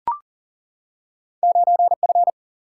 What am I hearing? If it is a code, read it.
Morse code, heard as a keyed tone, about 42 wpm, 700 Hz. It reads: 9F
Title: Single-Letter Single-Number 40wpm